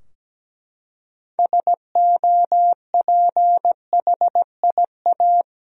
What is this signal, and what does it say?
Morse code, heard as a keyed tone, about 17 wpm, 700 Hz. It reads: SOPHIA
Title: USA Common Names - 17wpm